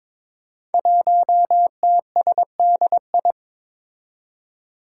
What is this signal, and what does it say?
Morse code, heard as a keyed tone, about 22 wpm, 700 Hz. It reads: 1TSDI
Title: Five Challenging Letter-Number Combinations 22wpm